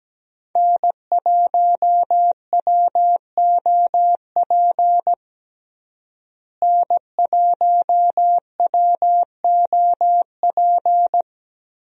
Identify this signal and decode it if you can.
Morse code, heard as a keyed tone, about 17 wpm, 700 Hz. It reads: N1WOP N1WOP
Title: USA Call Signs - 5 Characters - Repeated 3x - 17wpm